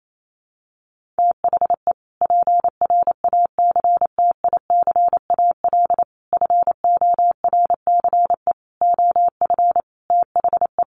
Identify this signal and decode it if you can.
Morse code, heard as a keyed tone, about 28 wpm, 700 Hz. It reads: THE PRACTICAL FORCE OF THE
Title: Sentences from Top 1400 Words 28wpm